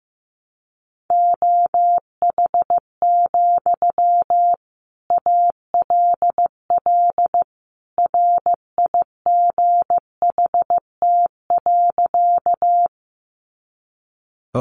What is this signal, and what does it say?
Morse code, heard as a keyed tone, about 15 wpm, 700 Hz. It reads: OH, ALL RIGHT.
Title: Sentences from Top 500 Words 15wpm